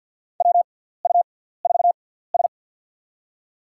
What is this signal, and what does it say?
Morse code, heard as a keyed tone, about 47 wpm, spaced as for 17 wpm, 700 Hz. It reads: WU4S
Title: USA Call Signs - 4 Characters - 6x Character Spacing - 50wpm